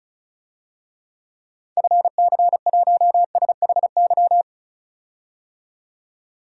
Morse code, heard as a keyed tone, about 36 wpm, 700 Hz. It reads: FC1SHY
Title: DXCC-100 Call Signs - 6 Characters - 35wpm